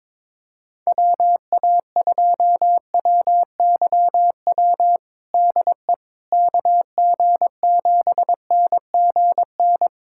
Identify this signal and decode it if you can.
Morse code, heard as a keyed tone, about 22 wpm, 700 Hz. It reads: WA2WYW DE KG7NGN
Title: USA Callsign DE Callsign - 6 Characters - 22wpm